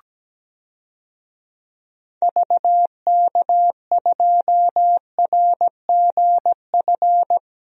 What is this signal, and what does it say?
Morse code, heard as a keyed tone, about 17 wpm, 700 Hz. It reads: VK2RGF
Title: DXCC-100 Call Signs - 6 Characters - 17wpm